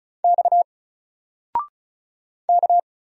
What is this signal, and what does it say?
Morse code, heard as a keyed tone, about 34 wpm, 700 Hz. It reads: X K
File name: Single-Letter-Number-35wpm